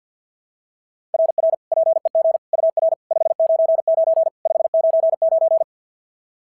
Morse code, heard as a keyed tone, about 51 wpm, 690 Hz. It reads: RR PEG UR 599 599
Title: SOTA QSO Practice 50wpm